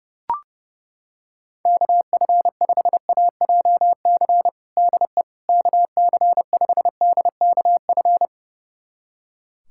Morse code, heard as a keyed tone, about 30 wpm, 700 Hz. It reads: KF5AJC DE KC5DKF
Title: USA Callsign DE Callsign - 6 Characters - 30wpm